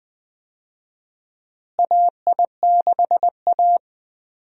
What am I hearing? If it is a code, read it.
Morse code, heard as a keyed tone, about 20 wpm, 700 Hz. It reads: AI6A
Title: USA Call Signs - 4 Characters - Repeated 2x - 20wpm